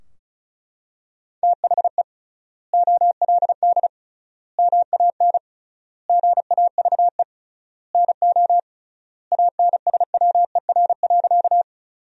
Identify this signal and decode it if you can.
Morse code, heard as a keyed tone, about 36 wpm, 700 Hz. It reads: THE OLD MAN GAVE NO ANSWER.
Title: 3x Word-Spacing - Sentences from Top 500 Words - 35wpm